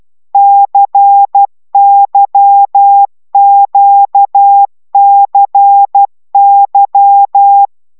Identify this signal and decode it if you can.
Morse code, heard as a keyed tone, about 12 wpm, 800 Hz. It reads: CYQCY
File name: taviro tanulas / cw 75 15 10